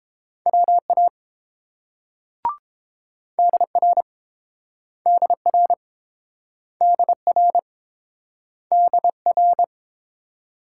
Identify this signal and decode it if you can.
Morse code, heard as a keyed tone, about 29 wpm, 700 Hz. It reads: WA DR DR DR DR